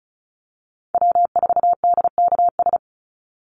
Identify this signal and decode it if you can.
Morse code, heard as a keyed tone, about 34 wpm, 700 Hz. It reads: W4DKS
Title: Call Signs - 35wpm